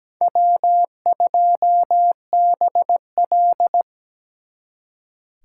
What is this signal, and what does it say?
Morse code, heard as a keyed tone, about 20 wpm, 700 Hz. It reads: W2BL
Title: Speed Racing - USA 1x2 Call Signs 17wpm